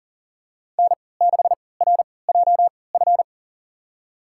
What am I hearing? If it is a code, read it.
Morse code, heard as a keyed tone, about 40 wpm, spaced as for 23 wpm, 700 Hz. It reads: N6RJF
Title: USA Call Signs - 5 Characters - 3x Character Spacing - 40wpm